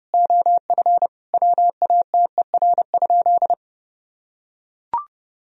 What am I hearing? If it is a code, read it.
Morse code, heard as a keyed tone, about 30 wpm, 700 Hz. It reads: OF WATER?
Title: Sentences from Top 1000 Words - Review Part 2 - 30wpm